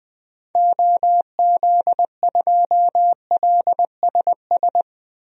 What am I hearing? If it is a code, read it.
Morse code, heard as a keyed tone, about 21 wpm, 700 Hz. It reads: OZ2LSS
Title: DXCC-100 Call Signs - 6 Characters - Repeated 3x - 20wpm